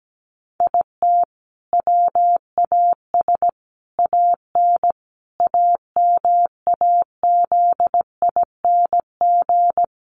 Morse code, heard as a keyed tone, about 17 wpm, 700 Hz. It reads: IT WAS AN AMAZING
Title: Sentences from Top 1500 Words 17wpm